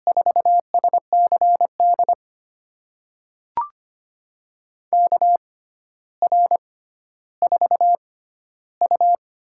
Morse code, heard as a keyed tone, about 25 wpm, spaced as for 13 wpm, 700 Hz. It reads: 4SCD KR4U
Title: USA Call Signs - 6x Character Spacing - 25wpm